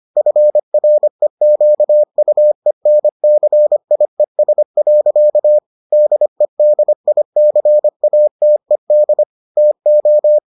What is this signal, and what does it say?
Morse code, heard as a keyed tone, about 25 wpm, 600 Hz. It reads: FREQUENCIES. DEDICATED TO